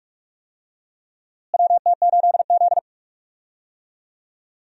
Morse code, heard as a keyed tone, about 46 wpm, 700 Hz. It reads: WT8Z
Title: USA Call Signs - 4 Characters - Repeated 3x - 45wpm